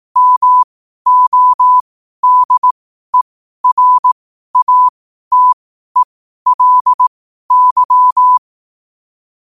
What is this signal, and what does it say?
Morse code, heard as a keyed tone, about 18 wpm, spaced as for 12 wpm, 1000 Hz. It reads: MODERATELY
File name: qotdfiles / 12wpm